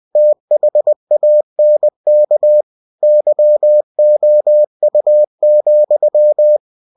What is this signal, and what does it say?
Morse code, heard as a keyed tone, about 20 wpm, 600 Hz. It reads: THANK YOU,
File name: cw-books / winnie-the-pooh / winnie-20wpm-page-0057